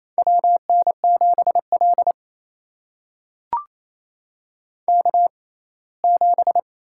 Morse code, heard as a keyed tone, about 28 wpm, spaced as for 19 wpm, 700 Hz. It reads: WN7L K7
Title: USA Call Signs - 6x Character Spacing - 28wpm